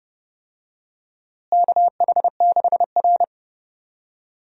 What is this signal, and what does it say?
Morse code, heard as a keyed tone, about 31 wpm, 700 Hz. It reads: KH6R
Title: USA Call Signs - 4 Characters - Repeated 3x - 30wpm